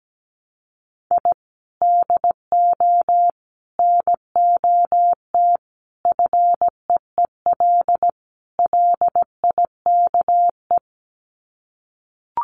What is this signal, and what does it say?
Morse code, heard as a keyed tone, about 17 wpm, 700 Hz. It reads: I DO NOT FEEL LIKE
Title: Sets of 5 Words Encore 17wpm